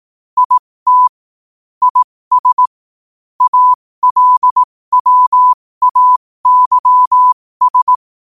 Morse code, heard as a keyed tone, about 18 wpm, spaced as for 15 wpm, 1000 Hz. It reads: IT IS ALWAYS
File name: qotdfiles / 15wpm